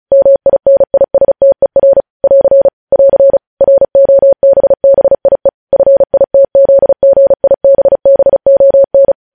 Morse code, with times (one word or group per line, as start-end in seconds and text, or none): 0.1-2.0 MINISTER
2.2-2.7 <AR>
2.9-3.4 <AR>
3.6-5.5 ROBBIE
5.7-9.1 FITZGIBBON